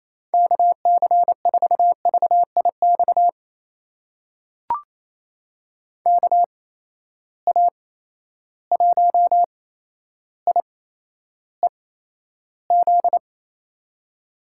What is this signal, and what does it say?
Morse code, heard as a keyed tone, about 28 wpm, spaced as for 12 wpm, 700 Hz. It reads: KC4VIX KA1IEZ
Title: USA Call Signs - 6 Characters - 8x Character Spacing - 28wpm